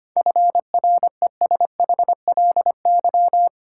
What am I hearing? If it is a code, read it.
Morse code, heard as a keyed tone, about 25 wpm, 700 Hz. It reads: FRESHLY
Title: Sets of 5 Words Encore - New Word Drill - 25wpm